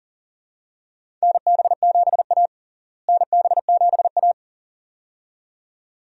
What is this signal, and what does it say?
Morse code, heard as a keyed tone, about 40 wpm, 700 Hz. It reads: NB7A NB7A
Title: USA Call Signs - 4 Characters - Repeated 2x - 40wpm